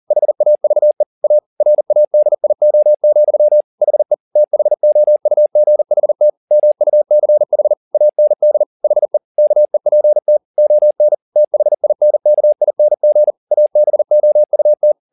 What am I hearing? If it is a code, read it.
Morse code, heard as a keyed tone, about 40 wpm, 600 Hz. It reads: HAVE A RADIO, HE THOUGHT MUCH AND HE KEPT ON THINKING ABOUT
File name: the old man and the sea 2 40wpm